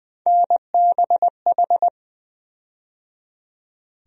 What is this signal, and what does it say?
Morse code, heard as a keyed tone, about 20 wpm, 700 Hz. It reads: NBH
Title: USA Call Sign Postfixes - 3 Letters 20wpm